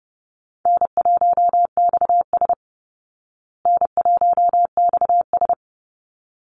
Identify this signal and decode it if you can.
Morse code, heard as a keyed tone, about 30 wpm, 700 Hz. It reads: N1XS N1XS
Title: Easy Call Signs Repeated 2x - 30wpm